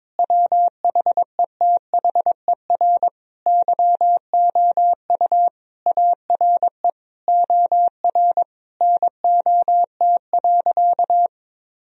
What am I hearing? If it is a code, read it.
Morse code, heard as a keyed tone, about 22 wpm, 700 Hz. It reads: WHETHER YOU ARE OR NOT.
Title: Sentences from Top 1000 Words - Review Part 3 - 22wpm